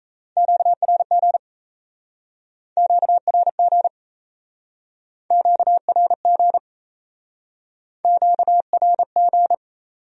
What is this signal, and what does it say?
Morse code, heard as a keyed tone, about 38 wpm, 700 Hz. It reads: QRG QRG QRG QRG